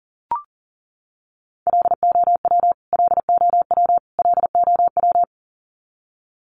Morse code, heard as a keyed tone, about 40 wpm, 700 Hz. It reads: LOW LOW LOW
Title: Top 300 Words Repeated 40wpm